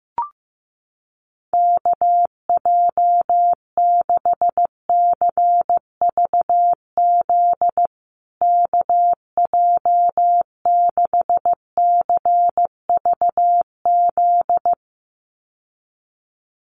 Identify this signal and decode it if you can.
Morse code, heard as a keyed tone, about 15 wpm, 700 Hz. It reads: KJ6CVZ KJ6CVZ
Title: USA Call Signs Repeated 2x 15wpm